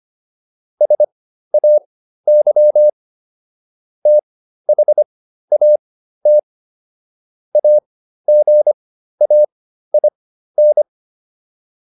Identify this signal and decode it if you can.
Morse code, heard as a keyed tone, about 25 wpm, spaced as for 13 wpm, 600 Hz. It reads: SAY THAT AGAIN
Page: intermediate-practice-files